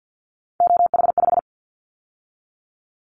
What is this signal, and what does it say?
Morse code, heard as a keyed tone, about 50 wpm, 700 Hz. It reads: OH5
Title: Callsign Prefixes for Top 50 DXCC - 50wpm